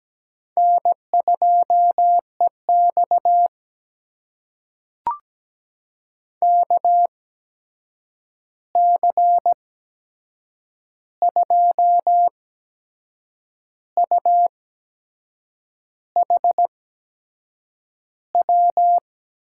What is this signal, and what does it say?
Morse code, heard as a keyed tone, about 17 wpm, spaced as for 6 wpm, 700 Hz. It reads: N2EX KC2UHW
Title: USA Call Signs - 8x Character Spacing - 17wpm